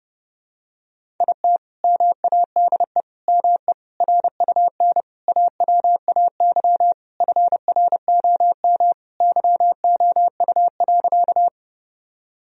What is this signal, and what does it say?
Morse code, heard as a keyed tone, about 30 wpm, 700 Hz. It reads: IT MADE ME RUN AWAY FROM YOU.
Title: Sentences from Top 1000 Words - Review Part 1 - 30wpm